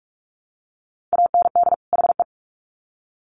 Morse code, heard as a keyed tone, about 44 wpm, 700 Hz. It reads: AND HE
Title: Sets of 2 Words 45wpm